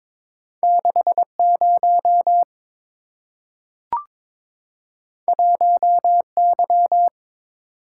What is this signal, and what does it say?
Morse code, heard as a keyed tone, about 22 wpm, 700 Hz. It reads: 60 1Y
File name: practice / 22wpm / Two Letter-Number Combinations - 22wpm